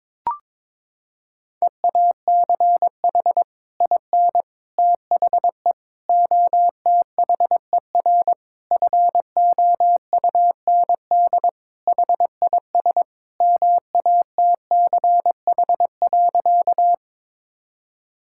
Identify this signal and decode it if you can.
Morse code, heard as a keyed tone, about 22 wpm, 700 Hz. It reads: EACH IN THE OTHER FOUND HIS MATCH.
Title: Sentences from Top 1600 Words 22wpm